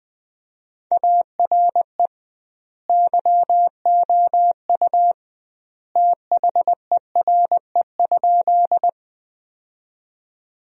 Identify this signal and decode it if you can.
Morse code, heard as a keyed tone, about 20 wpm, 700 Hz. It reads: ARE YOU THERE?
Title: 2x Word-Spacing - Sentences from Top 100 Words - 20wpm